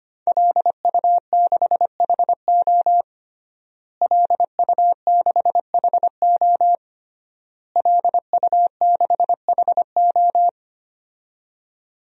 Morse code, heard as a keyed tone, about 25 wpm, 700 Hz. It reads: LU6HO LU6HO LU6HO
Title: DXCC-100 Call Signs - 5 Characters - Repeated 3x - 25wpm